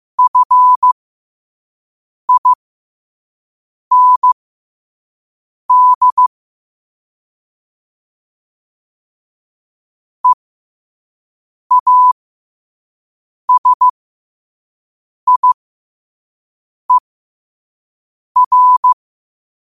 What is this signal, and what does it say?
Morse code, heard as a keyed tone, about 15 wpm, spaced as for 5 wpm, 1000 Hz. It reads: FIND EASIER
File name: qotdfiles / 5wpm